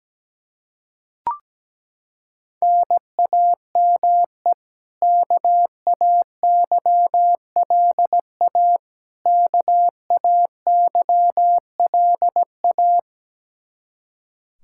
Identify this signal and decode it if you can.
Morse code, heard as a keyed tone, about 17 wpm, 700 Hz. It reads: NAME KAYLA KAYLA
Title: Practice QSO - Exchange 1 - Beginning - 17wpm